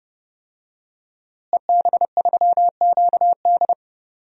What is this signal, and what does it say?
Morse code, heard as a keyed tone, about 32 wpm, 700 Hz. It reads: EB3QD
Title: DXCC-100 Call Signs - 5 Characters - Repeated 2x - 30wpm